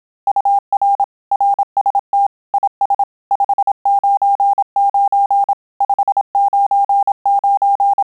Morse code, heard as a keyed tone, about 27 wpm, 800 Hz. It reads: UR RST IS 599 599
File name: Telegraphy-group / QSO CW 25 PPM